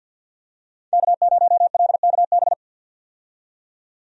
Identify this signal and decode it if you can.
Morse code, heard as a keyed tone, about 53 wpm, 700 Hz. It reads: K0LKB
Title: USA Call Signs - 5 Characters - 50wpm